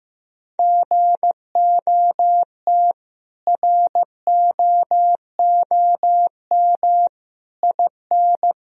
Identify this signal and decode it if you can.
Morse code, heard as a keyed tone, about 16 wpm, 700 Hz. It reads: GOT ROOM IN
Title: Sentences from Top 1000 Words - Review Part 1 - 15wpm